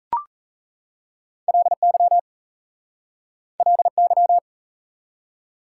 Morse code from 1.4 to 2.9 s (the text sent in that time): LY